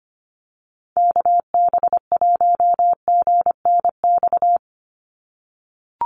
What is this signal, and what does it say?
Morse code, heard as a keyed tone, about 25 wpm, 700 Hz. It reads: KB1GNX
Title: USA Call Signs 25wpm